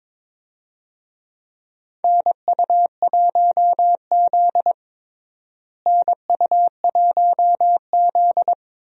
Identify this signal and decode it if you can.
Morse code, heard as a keyed tone, about 22 wpm, 700 Hz. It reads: NU1Z NU1Z
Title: USA Call Signs - 4 Characters - Repeated 2x - 22wpm